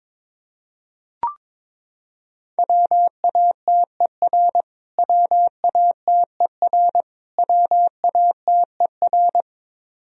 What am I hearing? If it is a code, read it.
Morse code, heard as a keyed tone, about 22 wpm, 700 Hz. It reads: WATER WATER WATER
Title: Top 100 Words Repeated 22wpm